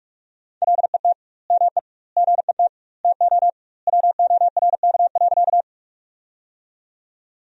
Morse code, heard as a keyed tone, about 46 wpm, 700 Hz. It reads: LET ME GET TO WORK.
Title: 2x Word-Spacing - Sentences from Top 300 Words - 45wpm